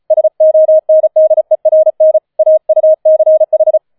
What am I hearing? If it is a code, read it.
Morse code, heard as a keyed tone, about 36 wpm, 630 Hz. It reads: SONDERN AUCH
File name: 004 Vierter Streich 35wpm